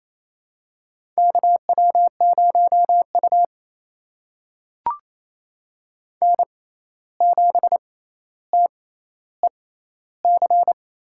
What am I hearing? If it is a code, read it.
Morse code, heard as a keyed tone, about 28 wpm, spaced as for 13 wpm, 700 Hz. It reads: KW0U N7TEC
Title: USA Call Signs - 6x Character Spacing - 28wpm